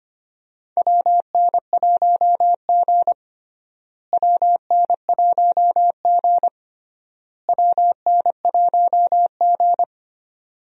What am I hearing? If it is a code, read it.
Morse code, heard as a keyed tone, about 25 wpm, 700 Hz. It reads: WN1G WN1G WN1G
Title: USA Call Signs - 4 Characters - Repeated 3x - 25wpm